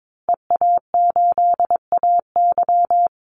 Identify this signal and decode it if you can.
Morse code, heard as a keyed tone, about 22 wpm, 700 Hz. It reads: EA8AY
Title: Call Signs Repeated 3x - 22wpm